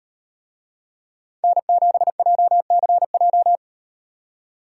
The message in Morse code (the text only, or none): N7JCJ